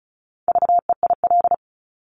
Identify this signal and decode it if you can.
Morse code, heard as a keyed tone, about 36 wpm, 700 Hz. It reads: VEIL
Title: Multi-letter Phonemes Repeated 3x 35wpm